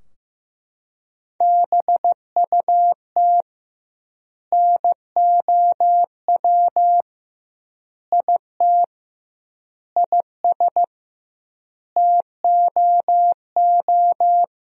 Morse code, heard as a keyed tone, about 15 wpm, 700 Hz. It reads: BUT NOW IT IS TOO
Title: 2x Word-Spacing - Sentences from Top 300 Words - 15wpm